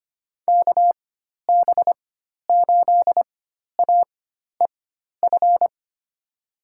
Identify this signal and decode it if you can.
Morse code, heard as a keyed tone, about 25 wpm, spaced as for 12 wpm, 700 Hz. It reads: KB8AEF
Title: USA Call Signs - 6 Characters - 4x Character Spacing - 25wpm